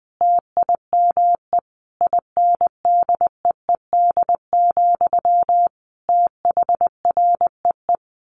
Morse code, heard as a keyed tone, about 20 wpm, 700 Hz. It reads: TIME INDEED, THREE
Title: A Sound Of Thunder 20wpm